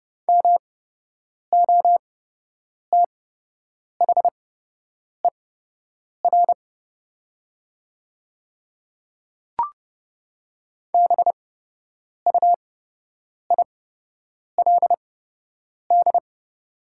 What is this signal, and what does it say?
Morse code, heard as a keyed tone, about 29 wpm, spaced as for 8 wpm, 690 Hz. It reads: MOTHER BUILD
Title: 8x Character Spacing - Top 200 Words - 30wpm